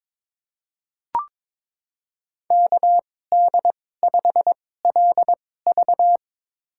Morse code, heard as a keyed tone, about 22 wpm, spaced as for 16 wpm, 700 Hz. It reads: KD5LV
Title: USA Call Signs - 5 Characters - 2x Character Spacing - 22wpm